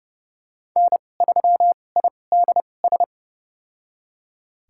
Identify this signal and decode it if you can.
Morse code, heard as a keyed tone, about 31 wpm, spaced as for 22 wpm, 700 Hz. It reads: N3IDS
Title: USA Call Signs - 5 Characters - 2x Character Spacing - 30wpm